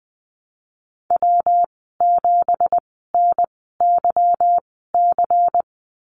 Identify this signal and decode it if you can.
Morse code, heard as a keyed tone, about 20 wpm, spaced as for 14 wpm, 700 Hz. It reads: W7NYC
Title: 173 Lesson 28g - Number 9 - Call Signs 20wpm